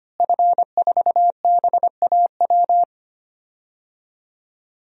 Morse code, heard as a keyed tone, about 25 wpm, 700 Hz. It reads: F4BAW
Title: DXCC-100 Call Signs - 5 Characters - Repeated 3x - 25wpm